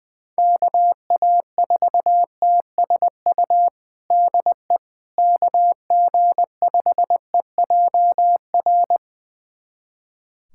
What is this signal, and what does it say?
Morse code, heard as a keyed tone, about 20 wpm, 700 Hz. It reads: KA4TSU DE KG5EJR
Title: USA Callsign DE Callsign - 6 Characters - 20wpm